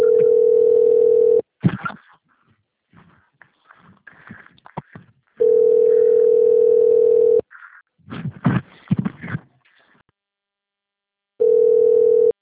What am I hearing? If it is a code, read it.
Ringback tone (North America), 440 + 480 Hz.